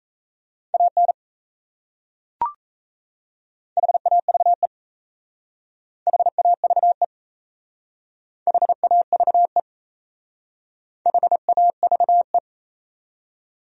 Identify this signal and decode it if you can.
Morse code, heard as a keyed tone, about 34 wpm, 700 Hz. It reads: AN HAVE HAVE HAVE HAVE
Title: Speed-Racing Top 100 Words 28wpm